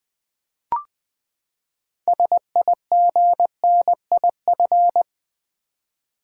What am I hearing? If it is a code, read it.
Morse code, heard as a keyed tone, about 20 wpm, 700 Hz. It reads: SIGNIF
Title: Build A Word - 6 Letters At A Time - 12-Letter Words - 20wpm